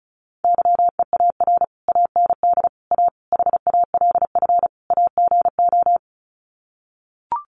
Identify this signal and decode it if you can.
Morse code, heard as a keyed tone, about 34 wpm, 700 Hz. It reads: YEAR AND A HALF AGO
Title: Sets of 5 Words Encore 35wpm